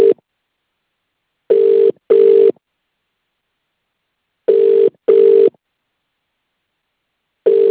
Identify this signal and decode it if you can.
Ringback tone (UK), 400 + 450 Hz.